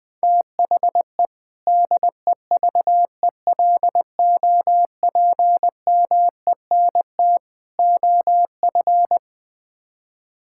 Morse code, heard as a keyed tone, about 20 wpm, 700 Hz. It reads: THE DEVELOPMENT OF
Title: Build A Phrase - 3-Word Phrases - 20wpm